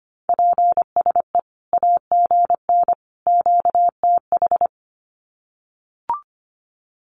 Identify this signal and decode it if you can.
Morse code, heard as a keyed tone, about 25 wpm, 700 Hz. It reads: PSE AGN QTH
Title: Common QSO Phrases 25wpm